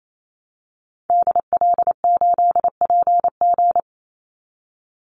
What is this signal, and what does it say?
Morse code, heard as a keyed tone, about 28 wpm, 700 Hz. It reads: DL8PG
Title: Call Signs Repeated 2x - 28wpm